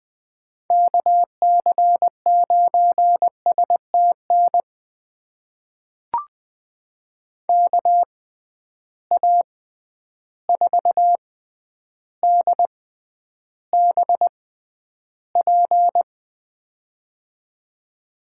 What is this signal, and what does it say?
Morse code, heard as a keyed tone, about 21 wpm, spaced as for 10 wpm, 700 Hz. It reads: KC9STN KA4DBP
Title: USA Call Signs - 6x Character Spacing - 20wpm